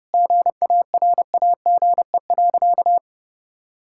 Morse code, heard as a keyed tone, about 30 wpm, 700 Hz. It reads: GARAGE.
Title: Sentences from Top 1600 Words 30wpm